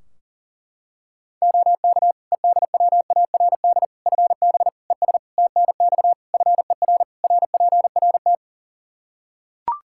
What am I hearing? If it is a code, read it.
Morse code, heard as a keyed tone, about 40 wpm, 700 Hz. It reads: OK EDWARD FB ES TNX FER RPRT
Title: Practice QSO - Exchange 2 - Info - 40wpm